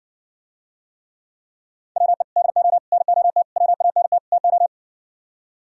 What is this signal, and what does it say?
Morse code, heard as a keyed tone, about 63 wpm, 700 Hz. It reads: WE DO NOT WANT TO